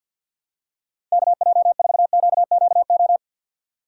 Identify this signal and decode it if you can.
Morse code, heard as a keyed tone, about 50 wpm, 700 Hz. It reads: KJ4QQO
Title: USA Call Signs - 6 Characters - 4x Character Spacing - 50wpm